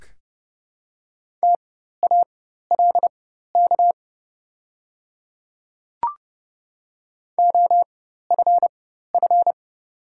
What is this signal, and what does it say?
Morse code, heard as a keyed tone, about 30 wpm, spaced as for 14 wpm, 700 Hz. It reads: TALK OFF
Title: 4x Character Spacing - Top 500 Words - 30wpm